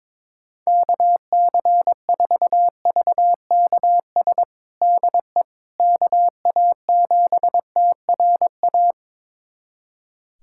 Morse code, heard as a keyed tone, about 22 wpm, 700 Hz. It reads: KC4VKS DE KA7TRA
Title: USA Callsign DE Callsign 22wpm